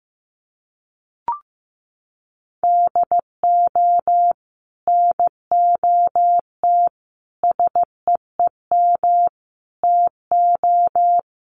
Morse code, heard as a keyed tone, about 15 wpm, 700 Hz. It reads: DO NOT SEEM TO
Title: Sets of 5 Words Encore2 15wpm